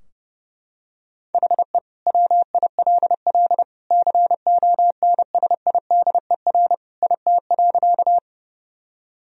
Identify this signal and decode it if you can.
Morse code, heard as a keyed tone, about 30 wpm, 700 Hz. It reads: HE WILL CONSIDER IT.
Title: Sentences from Top 1000 Words - Review Part 4 - 30wpm